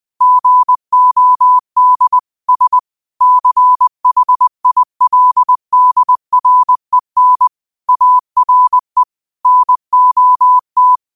Morse code, heard as a keyed tone, about 20 wpm, 1000 Hz. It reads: GOD S CHILDREN ARE NOT
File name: qotdfiles / 20wpm